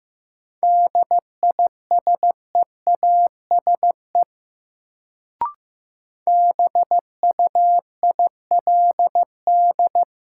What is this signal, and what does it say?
Morse code, heard as a keyed tone, about 15 wpm, 700 Hz. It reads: DISEASE BUILD
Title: Sentences from Top 1100 Words - New Word Drill Alt - 15wpm